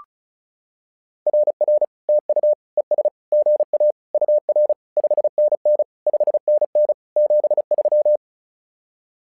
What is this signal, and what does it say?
Morse code, heard as a keyed tone, about 36 wpm, 600 Hz. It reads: RR TU ES GA UR 5NN 5NN 73
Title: SOTA QSO Practice 35wpm